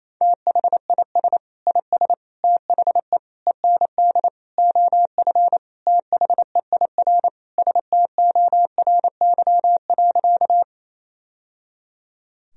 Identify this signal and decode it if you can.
Morse code, heard as a keyed tone, about 28 wpm, 700 Hz. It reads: THIS IS THE END OF THEIR STORY.
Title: Sentences from Top 1000 Words - Review Part 1 - 28wpm